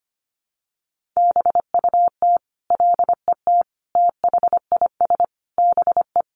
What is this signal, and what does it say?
Morse code, heard as a keyed tone, about 25 wpm, 700 Hz. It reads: BUT LET THIS BE
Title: Sentences from Top 1400 Words 25wpm